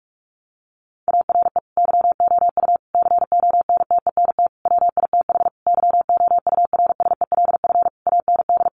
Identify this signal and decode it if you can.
Morse code, heard as a keyed tone, about 44 wpm, 700 Hz. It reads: ARE YOU CONTENT WITH YOURSELF AND
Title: Sentences from Top 900 Words 45wpm